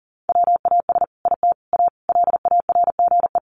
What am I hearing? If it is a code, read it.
Morse code, heard as a keyed tone, about 40 wpm, 700 Hz. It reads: WAS IT A LARGE
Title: Sentences from Top 1300 Words 40wpm